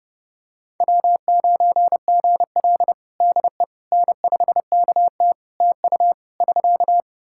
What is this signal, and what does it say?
Morse code, heard as a keyed tone, about 30 wpm, 700 Hz. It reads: W9GL DE N5KT TU <SK>
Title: Practice QSO - Exchange 4 - Ending - 30wpm